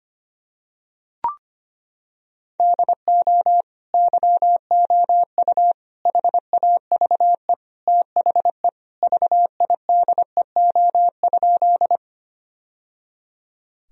Morse code, heard as a keyed tone, about 25 wpm, 700 Hz. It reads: DO YOU HAVE THE VIDEO?
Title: Sentences from Top 1000 Words - Review Part 4 - 25wpm